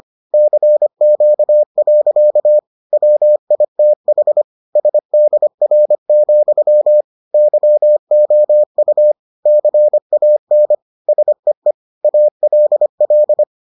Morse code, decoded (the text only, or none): CQ. WITH SDR, YOU CAN SEE ALL